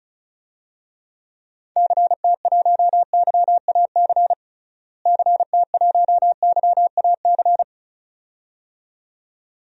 Morse code, heard as a keyed tone, about 34 wpm, 700 Hz. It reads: CT1YAC CT1YAC
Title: DXCC-100 Call Signs - 6 Characters - Repeated 2x - 35wpm